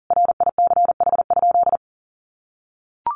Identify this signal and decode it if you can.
Morse code, heard as a keyed tone, about 40 wpm, 700 Hz. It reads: RICH?
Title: Sentences from Top 1100 Words 40wpm